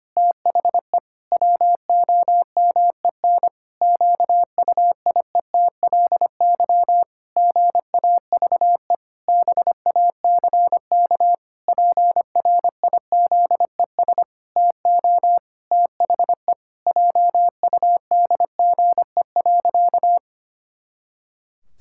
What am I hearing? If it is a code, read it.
Morse code, heard as a keyed tone, about 25 wpm, 700 Hz. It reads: THE WOMEN QUIETLY GAVE BACK PRIZES TO THE JUDGE.
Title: Pangram Challenge v2 - 25wpm